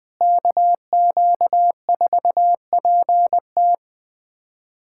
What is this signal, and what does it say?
Morse code, heard as a keyed tone, about 20 wpm, 700 Hz. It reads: KQ4PT
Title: USA Callsign DE Callsign - 5 Characters - 20wpm